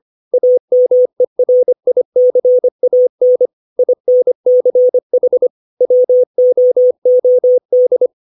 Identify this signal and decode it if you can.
Morse code, heard as a keyed tone, about 25 wpm, 500 Hz. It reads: AMERICAN INCH WOOD
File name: 632 words / 632 Words 25WPM